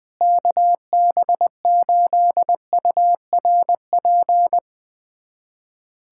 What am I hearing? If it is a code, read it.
Morse code, heard as a keyed tone, about 20 wpm, 700 Hz. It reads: KB8URP
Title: USA Callsign DE Callsign 20wpm